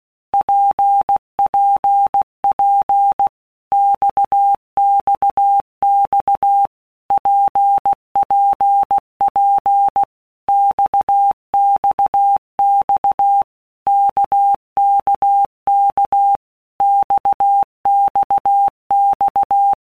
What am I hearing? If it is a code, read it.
Morse code, heard as a keyed tone, about 16 wpm, 800 Hz. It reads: PPP XXX PPP XXX KKK XXX
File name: Basic Letter Groups By Blocks / Groups@15 / RKPX@15